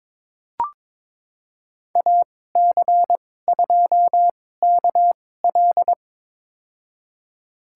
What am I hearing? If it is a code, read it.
Morse code, heard as a keyed tone, about 22 wpm, spaced as for 16 wpm, 700 Hz. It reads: AC2KL
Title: USA Call Signs - 2x Character Spacing - 22wpm